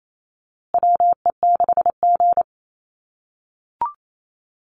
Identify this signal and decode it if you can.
Morse code, heard as a keyed tone, about 28 wpm, 700 Hz. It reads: WE6G
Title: USA Call Signs Repeated 3x 28wpm